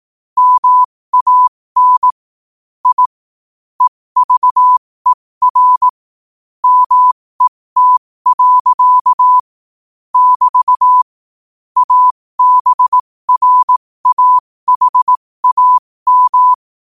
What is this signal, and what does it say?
Morse code, heard as a keyed tone, about 18 wpm, spaced as for 15 wpm, 1000 Hz. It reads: MAN I EVER MET. <BT> ABRAHAM
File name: qotdfiles / 15wpm